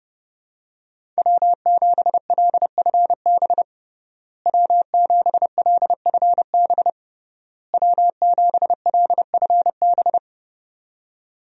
Morse code, heard as a keyed tone, about 30 wpm, 700 Hz. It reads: W7LFB W7LFB W7LFB
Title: USA Call Signs - 5 Characters - Repeated 3x - 30wpm